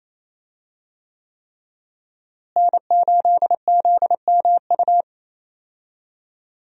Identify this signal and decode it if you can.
Morse code, heard as a keyed tone, about 27 wpm, 700 Hz. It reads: N8ZMU
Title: USA Call Signs - 5 Characters - Repeated 3x - 28wpm